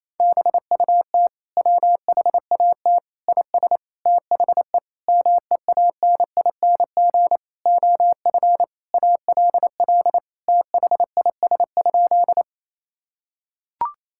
Morse code, heard as a keyed tone, about 27 wpm, 700 Hz. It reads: BUT WHAT IS THE MEANING OF ALL THIS?
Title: Sentences from Top 1000 Words - Review Part 6 - 28wpm